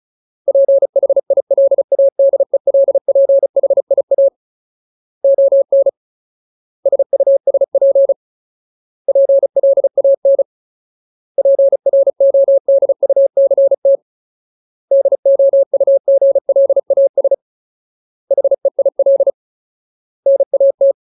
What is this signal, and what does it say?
Morse code, heard as a keyed tone, about 34 wpm, 550 Hz. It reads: PHILADELPHIA ON SUSP PLAN PRODUCT DOUGLAS HEIL NAT
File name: Ultimate Word List / 35WPM Ultimate